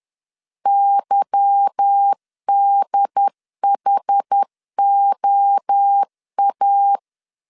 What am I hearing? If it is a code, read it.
Morse code, heard as a keyed tone, about 10 wpm, 800 Hz. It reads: YDHOA
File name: LEZIONI / 8 YJ45 / 8 Y 10wpm 2615